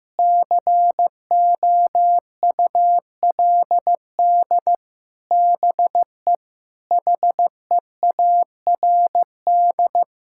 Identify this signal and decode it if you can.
Morse code, heard as a keyed tone, about 15 wpm, 700 Hz. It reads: COULD BE HEARD
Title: Sentences from Top 1000 Words - Review Part 5 - 15wpm